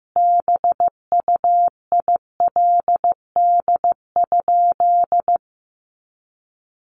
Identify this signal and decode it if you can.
Morse code, heard as a keyed tone, about 15 wpm, 700 Hz. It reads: BUILD?
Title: Sentences from Top 1300 Words 15wpm